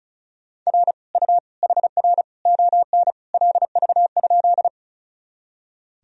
Morse code, heard as a keyed tone, about 36 wpm, 700 Hz. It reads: R U HR ON LV?
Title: Sentences from Top 400 Words and Abbreviations 35wpm